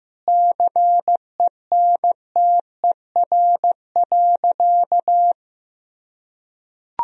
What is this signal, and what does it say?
Morse code, heard as a keyed tone, about 14 wpm, 700 Hz. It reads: CENTER.
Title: Sentences from Top 1000 Words - Review Part 2 - 15wpm